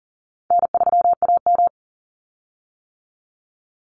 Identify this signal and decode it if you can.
Morse code, heard as a keyed tone, about 37 wpm, 700 Hz. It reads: N3AM
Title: Call Signs - 40wpm